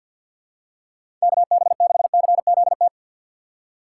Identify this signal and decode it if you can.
Morse code, heard as a keyed tone, about 46 wpm, 700 Hz. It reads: KB6CZT
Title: USA Call Signs - 6 Characters - 3x Character Spacing - 50wpm